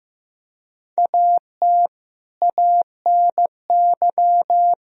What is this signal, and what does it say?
Morse code, heard as a keyed tone, about 15 wpm, 700 Hz. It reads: AT ANY
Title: Sentences from Top 1000 Words - Review Part 4 - 15wpm